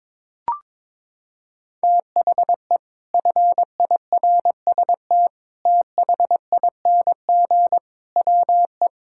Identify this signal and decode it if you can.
Morse code, heard as a keyed tone, about 22 wpm, 700 Hz. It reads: THE FIRST THING WE
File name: practice / 22wpm / Sentences with 8-Word Phrases - 22wpm